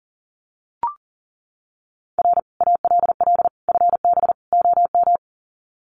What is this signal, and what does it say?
Morse code, heard as a keyed tone, about 40 wpm, 700 Hz. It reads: R ALL FB OM
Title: Common QSO Phrases 40wpm